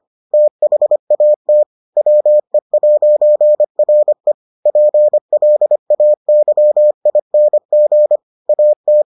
Morse code, heard as a keyed tone, about 25 wpm, 600 Hz. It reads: THAT WE'RE PLAYING AT